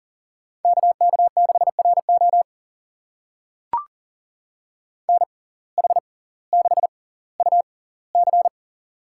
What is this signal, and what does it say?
Morse code, heard as a keyed tone, about 37 wpm, spaced as for 20 wpm, 700 Hz. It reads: KK6RO NH6UC
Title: USA Call Signs - 5 Characters - 6x Character Spacing - 40wpm